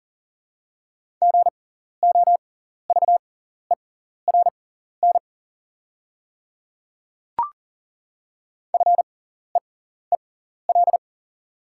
Morse code, heard as a keyed tone, about 40 wpm, spaced as for 14 wpm, 690 Hz. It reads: GOVERN FEEL
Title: 6x Character Spacing - Top 500 Words - 40wpm